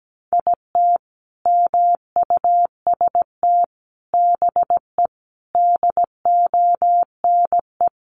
Morse code, heard as a keyed tone, about 17 wpm, 700 Hz. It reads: IT MUST BE DONE
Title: Sentences from Top 1100 Words 17wpm